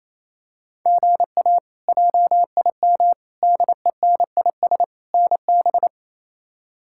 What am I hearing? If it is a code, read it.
Morse code, heard as a keyed tone, about 28 wpm, 700 Hz. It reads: GA JIM DENIS NB
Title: K1USN SST QSO Practice - 28wpm